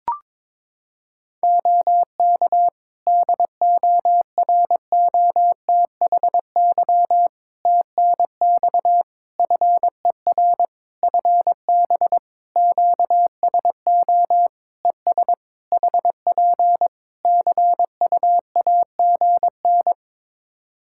Morse code, heard as a keyed tone, about 22 wpm, 700 Hz. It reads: OK DOROTHY TNX FER FB QSO ES HP CUAGN
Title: Practice QSO - Exchange 4 - Ending - 22wpm